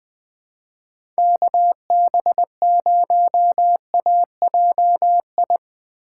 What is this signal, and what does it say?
Morse code, heard as a keyed tone, about 20 wpm, 700 Hz. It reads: KB0AJI
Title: USA Call Signs - 6 Characters - 8x Character Spacing - 20wpm